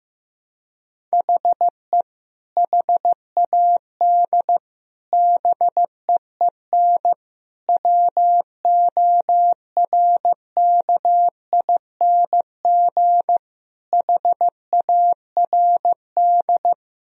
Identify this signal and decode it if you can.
Morse code, heard as a keyed tone, about 15 wpm, 700 Hz. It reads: HE HAD BEEN WORKING HARD
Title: Sentences from Top 1000 Words - Review Part 4 - 15wpm